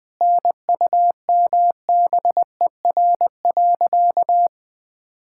Morse code, heard as a keyed tone, about 20 wpm, 700 Hz. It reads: NUMBER.
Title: Sentences with 5-Word Phrases - 20wpm